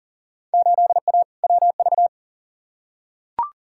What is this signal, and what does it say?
Morse code, heard as a keyed tone, about 40 wpm, 700 Hz. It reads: 8A WV
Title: Field Day Exchanges - 40wpm